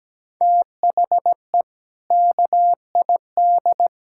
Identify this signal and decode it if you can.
Morse code, heard as a keyed tone, about 17 wpm, 700 Hz. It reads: THE KID
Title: Sentences from Top 1600 Words 17wpm